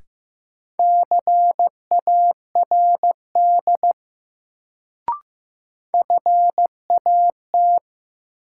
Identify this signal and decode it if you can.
Morse code, heard as a keyed tone, about 15 wpm, 700 Hz. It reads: CARD FAT
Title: Sentences from Top 600 Words - New Word Drill - 15wpm